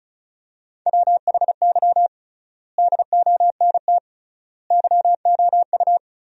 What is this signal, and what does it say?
Morse code, heard as a keyed tone, about 36 wpm, 700 Hz. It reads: WHY DONT YOU
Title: 3x Word-Spacing - Sentences from Top 300 Words - 35wpm